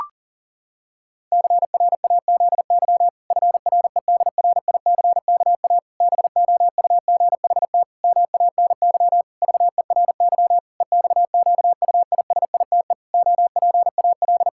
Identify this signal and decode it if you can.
Morse code, heard as a keyed tone, about 37 wpm, 700 Hz. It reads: CRAZY FREDRICKA BOUGHT MANY VERY EXQUISITE OPAL
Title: Pangram Challenge v2 - 40wpm